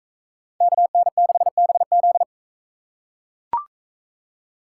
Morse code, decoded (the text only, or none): KN6BZ